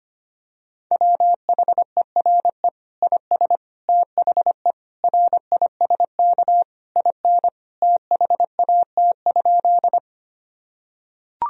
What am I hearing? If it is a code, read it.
Morse code, heard as a keyed tone, about 25 wpm, 700 Hz. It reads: WHERE IS THE RISK IN THAT?
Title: Sentences from Top 1000 Words - Review Part 4 - 25wpm